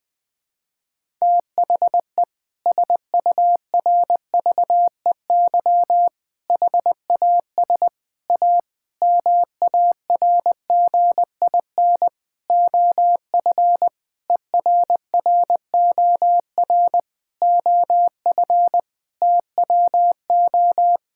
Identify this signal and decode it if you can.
Morse code, heard as a keyed tone, about 21 wpm, 700 Hz. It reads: THE SURVEY HAS A MARGIN OF ERROR OF TWO
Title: Sentences with 6-Word Phrases - Spoken Beginning - 20wpm